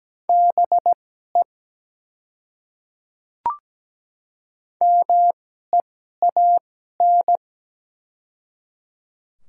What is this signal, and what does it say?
Morse code, heard as a keyed tone, about 17 wpm, spaced as for 12 wpm, 700 Hz. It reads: BE MEAN
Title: 2x Character Spacing - Top 200 Words - 17wpm